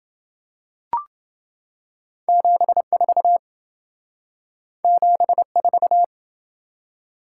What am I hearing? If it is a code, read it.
Morse code, heard as a keyed tone, about 28 wpm, 700 Hz. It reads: 74 74